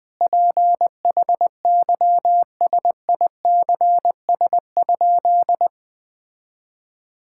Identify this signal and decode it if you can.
Morse code, heard as a keyed tone, about 20 wpm, 700 Hz. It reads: PHYSICS?
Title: Sentences from Top 1000 Words - Review Part 5 - 20wpm